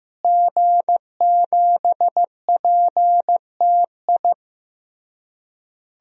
Morse code, heard as a keyed tone, about 15 wpm, 700 Hz. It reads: G7PTI
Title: DXCC-100 Call Signs - 5 Characters - Repeated 3x - 15wpm